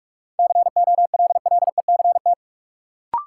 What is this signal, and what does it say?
Morse code, heard as a keyed tone, about 47 wpm, 700 Hz. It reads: COLLECT
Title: Sentences from Top 1600 Words - New Word Drill - 45wpm